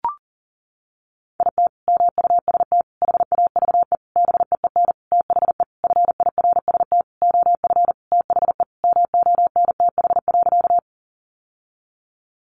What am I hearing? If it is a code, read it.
Morse code, heard as a keyed tone, about 40 wpm, 700 Hz. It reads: IT MUST HAVE BEEN THE FIRST OF THE MONTH.
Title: Sentences from Top 800 Words 40wpm